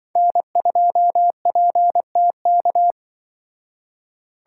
Morse code, heard as a keyed tone, about 24 wpm, 700 Hz. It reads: N2PTK
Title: Speed Racing - USA 1x3 Call Signs 20wpm